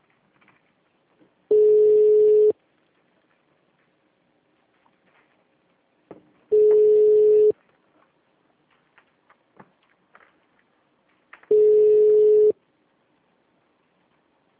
Ringback tone (Europe, 425 Hz).
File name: out- -2020- - - .3860